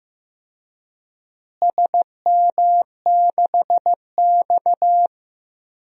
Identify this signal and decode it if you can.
Morse code, heard as a keyed tone, about 15 wpm, 700 Hz. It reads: SM6X
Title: DXCC-100 Call Signs - 4 Characters - Repeated 2x - 15wpm